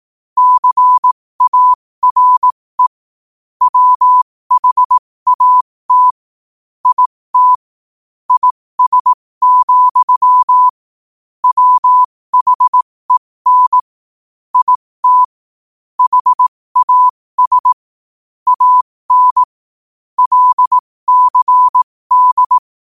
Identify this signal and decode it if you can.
Morse code, heard as a keyed tone, about 18 wpm, spaced as for 15 wpm, 1000 Hz. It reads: CARE WHAT IT IS, WHEN IT HAS AN LCD
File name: qotdfiles / 15wpm